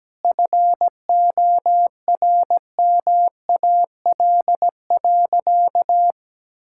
Morse code, heard as a keyed tone, about 17 wpm, 700 Hz. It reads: FORMAL.
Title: Sentences from Top 1600 Words 17wpm